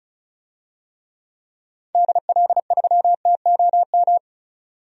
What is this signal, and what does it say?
Morse code, heard as a keyed tone, about 36 wpm, 700 Hz. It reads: DL3TOM
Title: DXCC-100 Call Signs - 6 Characters - 35wpm